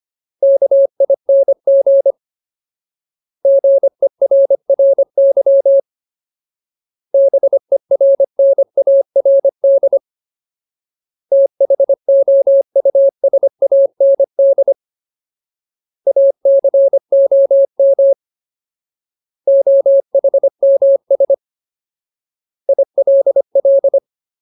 Morse code, heard as a keyed tone, about 25 wpm, 550 Hz. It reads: KING GERRY BERNARD THOUSAND ACOM OHMS ILL
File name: Ultimate Word List / 25WPM Ultimate